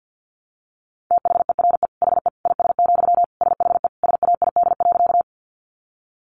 Morse code, heard as a keyed tone, about 48 wpm, 700 Hz. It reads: THERE HE IS, SHE SAID.
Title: Sentences from Top 100 Words 50wpm